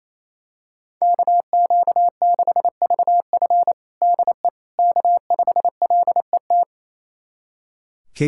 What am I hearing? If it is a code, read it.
Morse code, heard as a keyed tone, about 27 wpm, 700 Hz. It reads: KQ6VF DE K5LET
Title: USA Callsign DE Callsign - 5 Characters - 28wpm